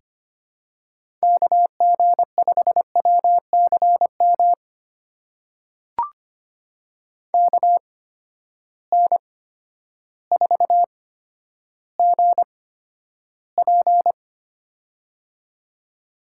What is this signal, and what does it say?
Morse code, heard as a keyed tone, about 25 wpm, spaced as for 11 wpm, 700 Hz. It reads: KG5WCM KN4GP
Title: USA Call Signs - 8x Character Spacing - 25wpm